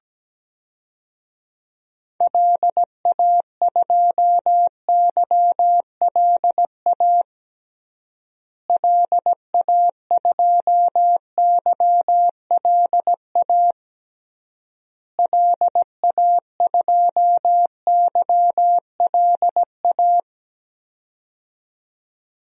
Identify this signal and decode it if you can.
Morse code, heard as a keyed tone, about 17 wpm, 700 Hz. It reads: LA2YLA LA2YLA LA2YLA
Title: DXCC-100 Call Signs - 6 Characters - Repeated 3x - 17wpm